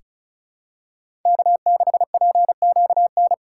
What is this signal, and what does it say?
Morse code, heard as a keyed tone, about 36 wpm, 700 Hz. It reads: K6PQN
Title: USA Callsign DE Callsign - 5 Characters - 35wpm